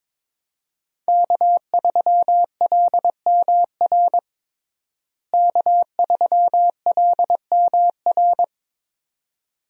Morse code, heard as a keyed tone, about 21 wpm, 700 Hz. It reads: K3LMR K3LMR
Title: USA Call Signs - 5 Characters - Repeated 2x - 22wpm